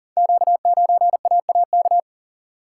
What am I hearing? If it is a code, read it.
Morse code, heard as a keyed tone, about 38 wpm, 690 Hz. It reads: Q9AAK
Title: Code Groups - 5 Letter:Numbers - 40wpm